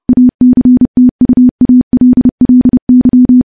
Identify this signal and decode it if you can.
Morse code, heard as a keyed tone, about 30 wpm, 250 Hz. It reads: ACTUALLY
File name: morse / morse-training-30WPM